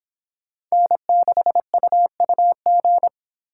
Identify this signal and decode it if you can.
Morse code, heard as a keyed tone, about 30 wpm, 700 Hz. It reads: N6UUG